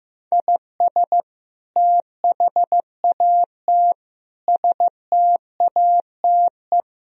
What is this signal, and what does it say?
Morse code, heard as a keyed tone, about 15 wpm, 700 Hz. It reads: IS THAT STATE
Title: Sentences from Top 1000 Words - Review Part 3 - 15wpm